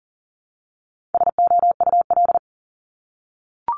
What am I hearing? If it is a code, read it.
Morse code, heard as a keyed tone, about 40 wpm, 700 Hz. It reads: SOUL
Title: Multi-letter Phonemes Repeated 3x 40wpm